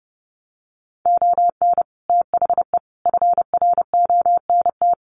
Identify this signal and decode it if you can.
Morse code, heard as a keyed tone, about 30 wpm, 700 Hz. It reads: ON THE FRONT
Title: Sets of 5 Words Encore 30wpm